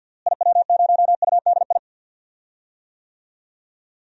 Morse code, heard as a keyed tone, about 53 wpm, 700 Hz. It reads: IW0UDI
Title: DXCC-100 Call Signs - 6 Characters - Repeated 2x - 50wpm